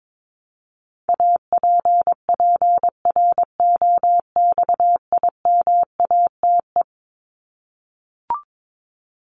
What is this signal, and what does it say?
Morse code, heard as a keyed tone, about 22 wpm, 700 Hz. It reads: APPROXIMATE
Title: LY Suffix - Top 100 Words with Base Words 22wpm